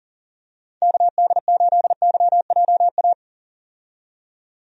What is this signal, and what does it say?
Morse code, heard as a keyed tone, about 37 wpm, 690 Hz. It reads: KD8YJA
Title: USA Call Signs - 6 Characters - 40wpm